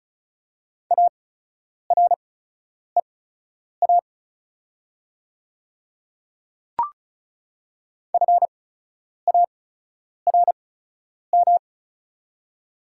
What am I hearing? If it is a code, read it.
Morse code, heard as a keyed tone, about 36 wpm, spaced as for 10 wpm, 690 Hz. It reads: AREA FARM
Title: 8x Character Spacing - Top 500 Words - 35wpm